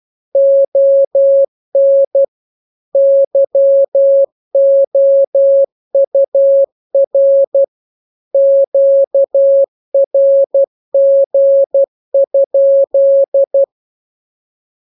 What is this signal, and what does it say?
Morse code, heard as a keyed tone, about 12 wpm, 550 Hz. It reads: ON YOUR QRG?
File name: jokes / joke 21 12 0000